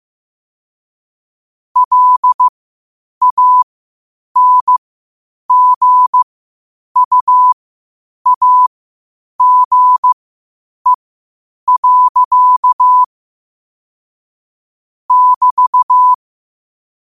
Morse code, heard as a keyed tone, about 15 wpm, spaced as for 8 wpm, 1000 Hz. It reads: LANGUAGE. <BT>
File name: qotdfiles / 8wpm